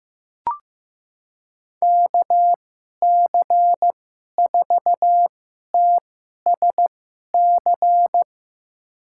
Morse code, heard as a keyed tone, about 15 wpm, spaced as for 11 wpm, 700 Hz. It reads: KC4TSC
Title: USA Call Signs - 6 Characters - 2x Character Spacing - 15wpm